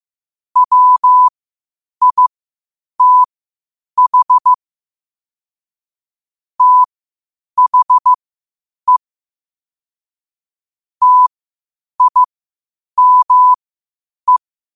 Morse code, heard as a keyed tone, about 15 wpm, spaced as for 8 wpm, 1000 Hz. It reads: WITH THE TIME
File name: qotdfiles / 8wpm